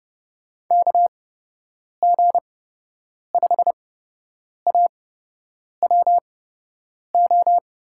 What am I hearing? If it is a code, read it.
Morse code, heard as a keyed tone, about 30 wpm, spaced as for 8 wpm, 700 Hz. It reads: KG5AWO
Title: USA Call Signs - 6 Characters - 8x Character Spacing - 30wpm